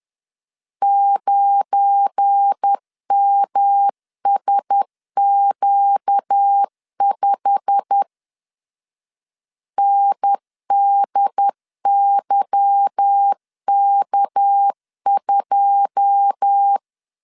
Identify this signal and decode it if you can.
Morse code, heard as a keyed tone, about 10 wpm, 800 Hz. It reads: 9MSQ5 NDYK2
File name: LEZIONI / 9 2 10wpm 2736